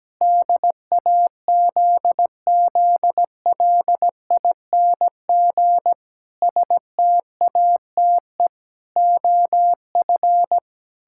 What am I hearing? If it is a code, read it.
Morse code, heard as a keyed tone, about 17 wpm, 700 Hz. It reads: DAZZLING STATE OF
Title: Pangram Challenge v2 - 17wpm